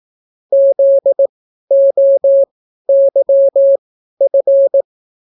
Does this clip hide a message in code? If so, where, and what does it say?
Yes: Morse code from 0.5 to 4.8 s, heard as a keyed tone, about 18 wpm, spaced as for 12 wpm, 550 Hz. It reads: ZOYF